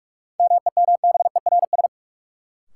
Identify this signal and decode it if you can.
Morse code, heard as a keyed tone, about 46 wpm, 700 Hz. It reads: MEMBERS
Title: Sets of 3 Words - New Word Drill - 45wpm